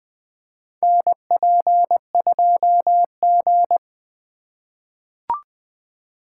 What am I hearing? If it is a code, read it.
Morse code, heard as a keyed tone, about 20 wpm, 700 Hz. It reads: NP2G
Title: USA Call Signs - 4 Characters - 4x Character Spacing - 20wpm